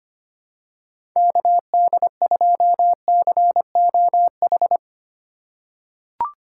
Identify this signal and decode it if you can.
Morse code, heard as a keyed tone, about 25 wpm, 700 Hz. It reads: KD2COH
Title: USA Call Signs - 2x Character Spacing - 25wpm